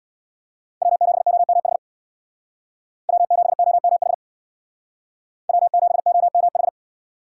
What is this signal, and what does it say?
Morse code, heard as a keyed tone, about 63 wpm, 700 Hz. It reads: W7OMH W7OMH W7OMH